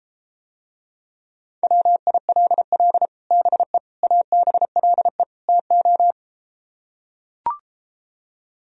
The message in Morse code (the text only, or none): WILL BE ABLE TO